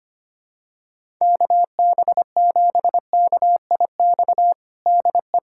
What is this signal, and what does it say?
Morse code, heard as a keyed tone, about 25 wpm, 700 Hz. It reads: KB7KIX DE
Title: USA Callsign DE Callsign 25wpm